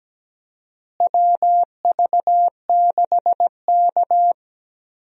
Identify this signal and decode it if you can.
Morse code, heard as a keyed tone, about 17 wpm, 700 Hz. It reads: WV6K
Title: USA Call Signs - 4 Characters - Repeated 2x - 17wpm